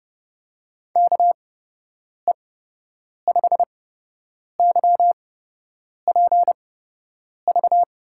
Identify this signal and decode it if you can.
Morse code, heard as a keyed tone, about 32 wpm, spaced as for 8 wpm, 700 Hz. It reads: KE5YPV
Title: USA Call Signs - 6 Characters - 8x Character Spacing - 30wpm